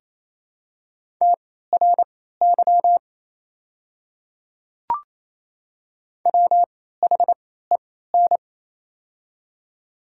Morse code, heard as a keyed tone, about 27 wpm, spaced as for 16 wpm, 690 Hz. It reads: TRY WHEN
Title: 3x Character Spacing - Top 200 Words - 28wpm